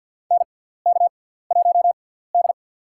Morse code, heard as a keyed tone, about 51 wpm, spaced as for 17 wpm, 700 Hz. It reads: NK1D
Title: USA Call Signs - 4 Characters - 6x Character Spacing - 50wpm